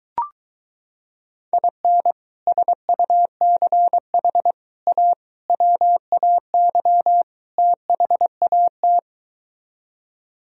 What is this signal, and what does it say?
Morse code, heard as a keyed tone, about 23 wpm, 700 Hz. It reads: IN SUCH A WAY THAT